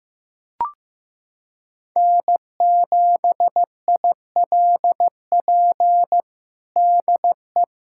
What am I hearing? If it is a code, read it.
Morse code, heard as a keyed tone, about 15 wpm, 700 Hz. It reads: N7ILP DE
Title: USA Callsign DE Callsign - 5 Characters - 15wpm